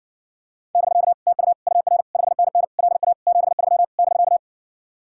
Morse code, heard as a keyed tone, about 60 wpm, 700 Hz. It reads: <BK> TU UR 5NN LA 73 <BK>